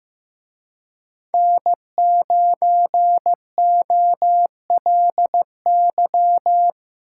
Morse code, heard as a keyed tone, about 18 wpm, 700 Hz. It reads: N9OLY
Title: Speed Racing - USA Call Signs 15wpm